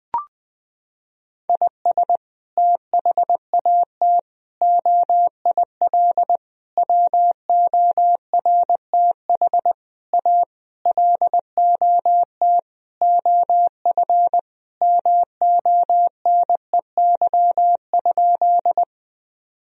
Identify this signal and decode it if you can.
Morse code, heard as a keyed tone, about 20 wpm, 700 Hz. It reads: IS THAT OIL WORTH A LOT OF MONEY?
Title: Sentences from Top 1000 Words - Review Part 4 - 20wpm